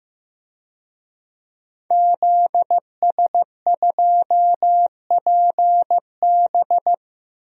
Morse code, heard as a keyed tone, about 15 wpm, 700 Hz. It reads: ZS2PB
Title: DXCC-100 Call Signs - 5 Characters - Repeated 2x - 15wpm